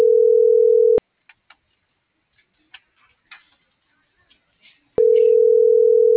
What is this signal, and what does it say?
Ringback tone (North America), 440 + 480 Hz.